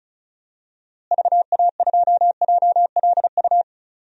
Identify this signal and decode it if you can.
Morse code, heard as a keyed tone, about 36 wpm, 700 Hz. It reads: VA2JLU
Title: DXCC-100 Call Signs - 6 Characters - Repeated 3x - 35wpm